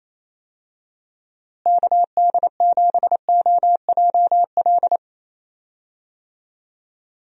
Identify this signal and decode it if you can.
Morse code, heard as a keyed tone, about 28 wpm, 700 Hz. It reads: KD7OJL
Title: USA Call Signs - 6 Characters - Repeated 2x - 28wpm